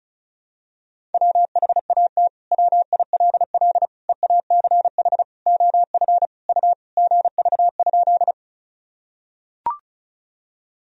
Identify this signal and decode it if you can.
Morse code, heard as a keyed tone, about 34 wpm, 690 Hz. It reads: WHAT WILL EACH OF U GV?
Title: Sentences from Top 200 Words and Abbreviations 35wpm